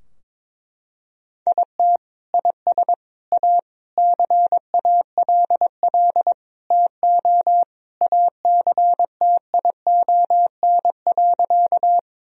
Morse code, heard as a keyed tone, about 22 wpm, 700 Hz. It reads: IT IS A CALL TO ACTION.
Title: Sentences from Top 1000 Words - Review Part 4 - 22wpm